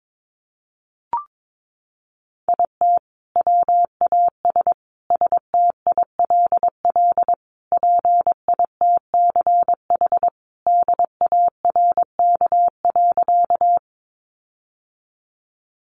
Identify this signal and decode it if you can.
Morse code, heard as a keyed tone, about 22 wpm, 700 Hz. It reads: IT WAS STILL PITCH DARK.
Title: Sentences from Top 1300 Words 22wpm